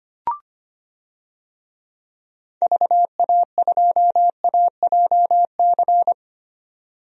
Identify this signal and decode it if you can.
Morse code, heard as a keyed tone, about 25 wpm, 700 Hz. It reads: VA2AJC
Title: DXCC-100 Call Signs - 6 Characters - 25wpm